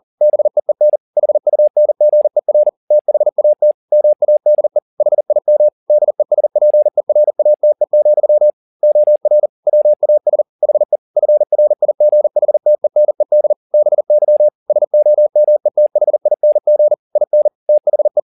Morse code, heard as a keyed tone, about 40 wpm, 600 Hz. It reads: BEEN HUNGER THAT MADE HIM DESPERATE, OR WAS HE FRIGHTENED BY SOMETHING IN THE
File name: the old man and the sea 2 40wpm